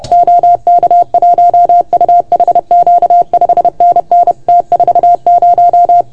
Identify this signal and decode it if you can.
Morse code, heard as a keyed tone, about 29 wpm, 690 Hz. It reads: OK1UHQ 5NN T40
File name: Archiv / CUC381 / OK1FGD CUC 381